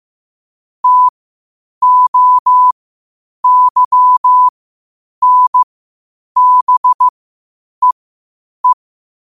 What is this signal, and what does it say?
Morse code, heard as a keyed tone, about 15 wpm, spaced as for 8 wpm, 1000 Hz. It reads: TOYNBEE
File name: qotdfiles / 8wpm